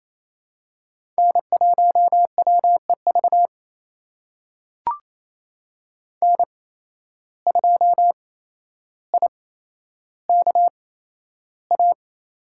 Morse code, heard as a keyed tone, about 28 wpm, spaced as for 12 wpm, 700 Hz. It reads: N1WEV N2IKA
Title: USA Call Signs - 5 Characters - 8x Character Spacing - 28wpm